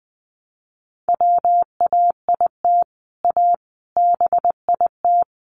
Morse code, heard as a keyed tone, about 20 wpm, 700 Hz. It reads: WAIT A BIT
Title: Sentences from Top 900 Words 20wpm